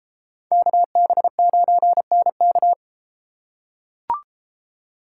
Morse code, heard as a keyed tone, about 33 wpm, 700 Hz. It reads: KB9NK